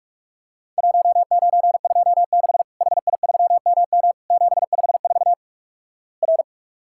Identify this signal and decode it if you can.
Morse code, heard as a keyed tone, about 47 wpm, 700 Hz. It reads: 1926 HI3MM 754 R
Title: Worked All Europe DX Contest - QTC Exchange - 45wpm